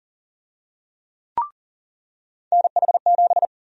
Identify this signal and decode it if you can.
Morse code, heard as a keyed tone, about 40 wpm, 700 Hz. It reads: NH7
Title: USA Call Sign Prefixes - 40wpm